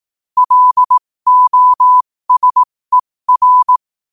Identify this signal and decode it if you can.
Morse code, heard as a keyed tone, about 18 wpm, spaced as for 15 wpm, 1000 Hz. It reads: LOSER
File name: qotdfiles / 15wpm